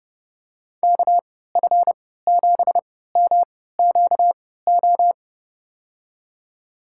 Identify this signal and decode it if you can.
Morse code, heard as a keyed tone, about 31 wpm, spaced as for 17 wpm, 700 Hz. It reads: KF7MQO
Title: USA Call Signs - 6 Characters - 3x Character Spacing - 30wpm